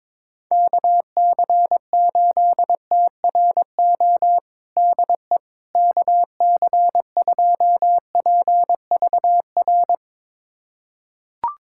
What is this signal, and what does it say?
Morse code, heard as a keyed tone, about 22 wpm, 700 Hz. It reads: KC8TRO DE KC2PVR
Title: USA Callsign DE Callsign - 6 Characters - 22wpm